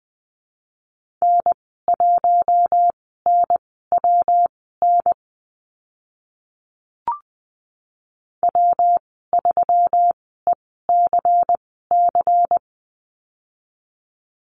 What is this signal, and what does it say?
Morse code, heard as a keyed tone, about 20 wpm, spaced as for 14 wpm, 700 Hz. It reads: N1NWN W3ECC
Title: 122 Lesson 20g - Number 3 - Call Signs 20wpm